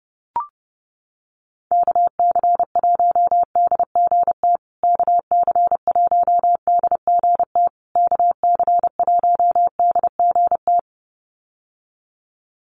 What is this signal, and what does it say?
Morse code, heard as a keyed tone, about 30 wpm, 700 Hz. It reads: KC1DGT KC1DGT KC1DGT
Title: USA Call Signs Repeated 3x 30wpm